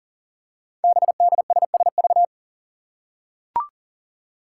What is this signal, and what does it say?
Morse code, heard as a keyed tone, about 42 wpm, 700 Hz. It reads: BDSSV
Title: Five Challenging Letter-Number Combinations 40wpm